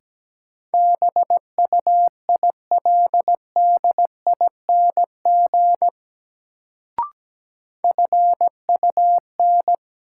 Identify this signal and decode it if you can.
Morse code, heard as a keyed tone, about 17 wpm, 700 Hz. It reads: BUILDING FUN
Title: Sentences from Top 700 Words - New Word Drill - 17wpm